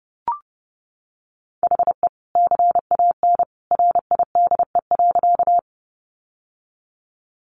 Morse code, heard as a keyed tone, about 30 wpm, 700 Hz. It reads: HE CAN RIDE.
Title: Sentences from Top 1200 Words 30wpm